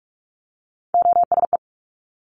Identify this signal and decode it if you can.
Morse code, heard as a keyed tone, about 46 wpm, 700 Hz. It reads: OSE
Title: Top Three Letter Combinations in Words 45wpm